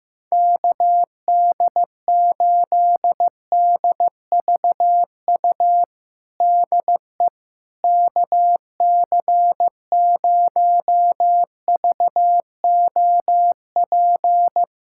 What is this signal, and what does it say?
Morse code, heard as a keyed tone, about 15 wpm, 700 Hz. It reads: KD8DVU DE KC0VOP
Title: USA Callsign DE Callsign 15wpm